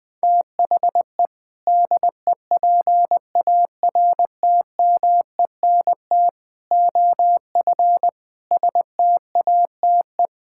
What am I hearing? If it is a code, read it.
Morse code, heard as a keyed tone, about 19 wpm, 700 Hz. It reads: THE DEPARTMENT OF STATE
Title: Sentences from Top 1000 Words - Review Part 5 - 20wpm